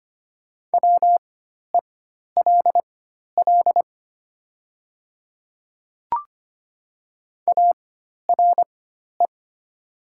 Morse code, heard as a keyed tone, about 25 wpm, spaced as for 12 wpm, 700 Hz. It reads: WELL ARE
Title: 4x Character Spacing - Top 300 Words - 25wpm